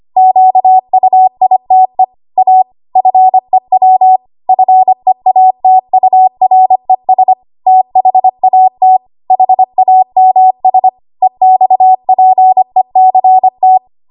Morse code, heard as a keyed tone, about 25 wpm, 750 Hz. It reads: QUITE A FEW FEATURES THAT HAMS EXPECT
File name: Archive / 25 WPM / 25WPM